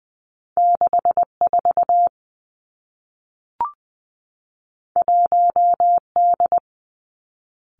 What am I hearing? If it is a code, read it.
Morse code, heard as a keyed tone, about 20 wpm, 700 Hz. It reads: 6 4 1 D
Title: Two Challenging Letter-Number Combinations 20wpm